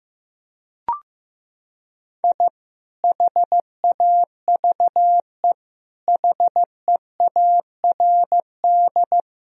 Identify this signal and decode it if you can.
Morse code, heard as a keyed tone, about 15 wpm, 700 Hz. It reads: I HAVE HEARD
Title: Sentences from Top 1000 Words - Review Part 3 - 15wpm